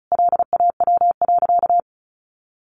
Morse code, heard as a keyed tone, about 34 wpm, 700 Hz. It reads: LAW.
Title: Sentences from Top 1400 Words 35wpm